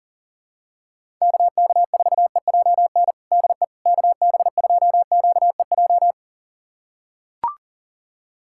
Morse code, heard as a keyed tone, about 40 wpm, 700 Hz. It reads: KK4EJN DE KB2QEJ
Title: USA Callsign DE Callsign 40wpm